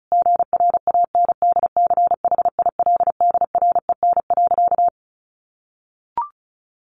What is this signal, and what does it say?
Morse code, heard as a keyed tone, about 34 wpm, 700 Hz. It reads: GRANDCHILDREN.
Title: A Sound Of Thunder 35wpm